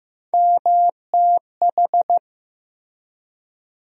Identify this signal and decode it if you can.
Morse code, heard as a keyed tone, about 15 wpm, 700 Hz. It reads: MTH
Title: USA Call Sign Postfixes - 3 Letters 15wpm